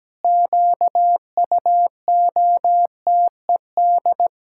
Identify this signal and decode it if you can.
Morse code, heard as a keyed tone, about 17 wpm, 700 Hz. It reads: QUOTED
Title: Pangram Challenge v2 - 17wpm